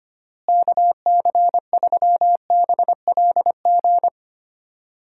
Morse code, heard as a keyed tone, about 25 wpm, 700 Hz. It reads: KC3BLG
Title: USA Call Signs - 6 Characters - Repeated 3x - 25wpm